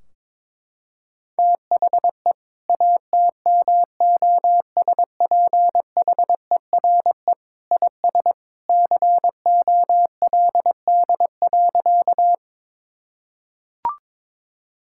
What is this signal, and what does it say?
Morse code, heard as a keyed tone, about 22 wpm, 700 Hz. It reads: THE ATMOSPHERE IS COLD.
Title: Sentences from Top 1600 Words 22wpm